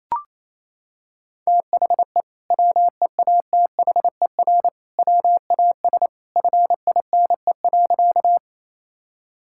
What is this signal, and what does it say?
Morse code, heard as a keyed tone, about 28 wpm, 700 Hz. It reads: THE WEATHER WAS FINE.
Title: Sentences from Top 1000 Words - Review Part 5 - 28wpm